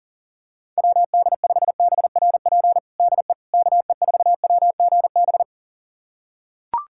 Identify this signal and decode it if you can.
Morse code, heard as a keyed tone, about 40 wpm, 700 Hz. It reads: WD5BRP DE KE4WGB
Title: USA Callsign DE Callsign 40wpm